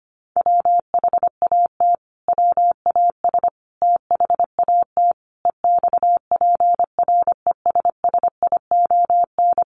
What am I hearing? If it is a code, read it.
Morse code, heard as a keyed tone, about 25 wpm, 700 Hz. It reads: WHAT WAS THAT EXPRESSION
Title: Sentences from Top 1200 Words 25wpm